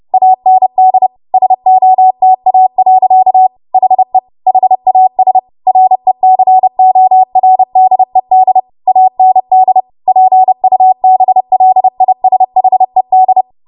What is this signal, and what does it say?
Morse code, heard as a keyed tone, about 30 wpm, 750 Hz. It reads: AND SOTA. HE HAS RECORDED AND PUBLISHED
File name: Morse / 30WPM